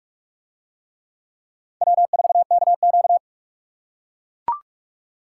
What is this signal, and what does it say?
Morse code, heard as a keyed tone, about 42 wpm, 700 Hz. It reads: W4KQ